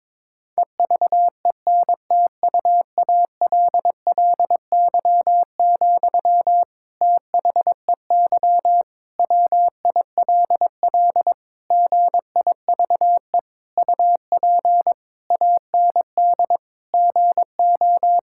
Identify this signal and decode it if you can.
Morse code, heard as a keyed tone, about 22 wpm, 700 Hz. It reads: EVENTUALLY, THEY WILL GIVE UP AND GO
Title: Sentences from Top 1000 Words - Review Part 6 - 22wpm